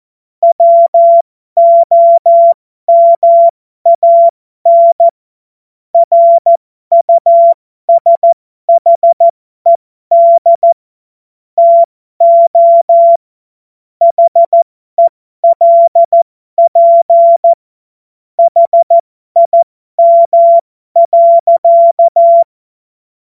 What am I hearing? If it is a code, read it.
Morse code, heard as a keyed tone, about 14 wpm, spaced as for 12 wpm, 680 Hz. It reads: WOMAN RUSHED TO HELP HIM.
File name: Morsekurs / Witze